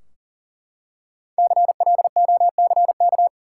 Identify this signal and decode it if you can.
Morse code, heard as a keyed tone, about 40 wpm, 700 Hz. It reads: CLOCK
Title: Five Letter Words - 40wpm